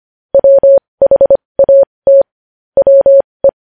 Morse code, heard as a keyed tone, about 25 wpm, spaced as for 20 wpm, 550 Hz. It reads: WHAT WE
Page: News Headlines in Morse Code at 20 WPM